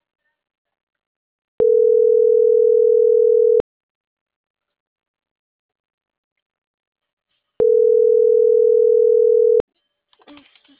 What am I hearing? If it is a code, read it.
Ringback tone (North America), 440 + 480 Hz.